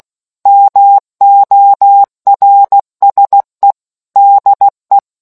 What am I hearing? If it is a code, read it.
Morse code, heard as a keyed tone, about 16 wpm, 800 Hz. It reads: MORSE DE